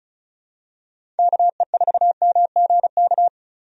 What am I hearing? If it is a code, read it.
Morse code, heard as a keyed tone, about 36 wpm, 700 Hz. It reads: KE4MGK
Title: USA Call Signs - 6 Characters - 35wpm